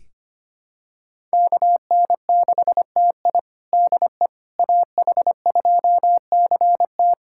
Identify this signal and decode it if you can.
Morse code, heard as a keyed tone, about 25 wpm, 700 Hz. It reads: KN6TI DE AH2CT
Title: USA Callsign DE Callsign - 5 Characters - 25wpm